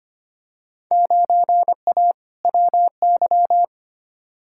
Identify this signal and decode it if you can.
Morse code, heard as a keyed tone, about 25 wpm, 700 Hz. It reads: 9A WY
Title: Field Day Exchanges Encore - 25wpm